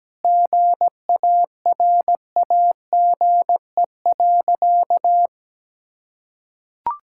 Morse code, heard as a keyed tone, about 17 wpm, 700 Hz. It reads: GARAGE.
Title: Sentences from Top 1600 Words 17wpm